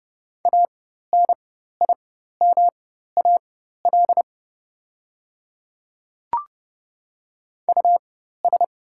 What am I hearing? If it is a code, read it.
Morse code, heard as a keyed tone, about 31 wpm, spaced as for 14 wpm, 700 Hz. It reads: ANIMAL US
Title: 4x Character Spacing - Top 300 Words - 30wpm